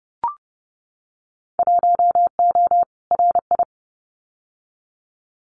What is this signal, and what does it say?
Morse code, heard as a keyed tone, about 30 wpm, 700 Hz. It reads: 1O RI
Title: Winter Field Day Exchanges 30wpm